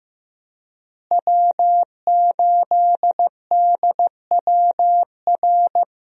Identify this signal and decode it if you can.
Morse code, heard as a keyed tone, about 15 wpm, 700 Hz. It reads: W8DWR
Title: USA Call Signs - 5 Characters - Repeated 3x - 15wpm